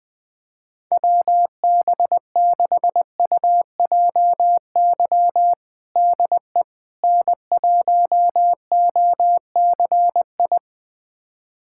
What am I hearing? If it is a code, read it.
Morse code, heard as a keyed tone, about 20 wpm, 700 Hz. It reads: WB6UJY DE N1OCI
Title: USA Callsign DE Callsign 20wpm